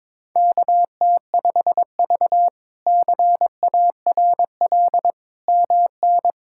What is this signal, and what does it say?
Morse code, heard as a keyed tone, about 22 wpm, 700 Hz. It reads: KT5V CARL MN
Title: K1USN SST QSO Practice - 22wpm